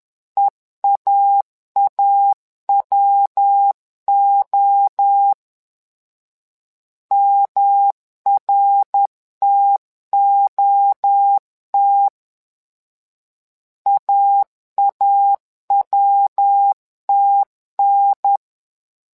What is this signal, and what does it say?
Morse code, heard as a keyed tone, about 10 wpm, 800 Hz. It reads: EAAWO MRTOT AAWTN
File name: LEZIONI / 3 MWRO / 3 MWRO 10wpm 2148